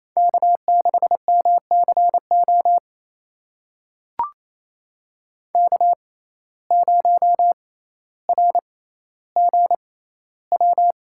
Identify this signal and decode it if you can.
Morse code, heard as a keyed tone, about 28 wpm, spaced as for 14 wpm, 700 Hz. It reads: K6MCO K0RGW
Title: USA Call Signs - 5 Characters - 6x Character Spacing - 28wpm